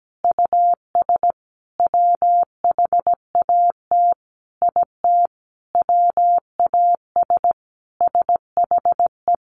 Morse code, heard as a keyed tone, about 17 wpm, 700 Hz. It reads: US WHAT IT WAS SHE
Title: Sentences from Top 400 Words 17wpm